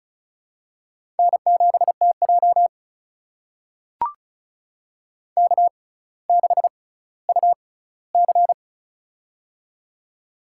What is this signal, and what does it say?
Morse code, heard as a keyed tone, about 36 wpm, spaced as for 18 wpm, 700 Hz. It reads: N7TJ K6UC
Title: USA Call Signs - 4 Characters - 6x Character Spacing - 35wpm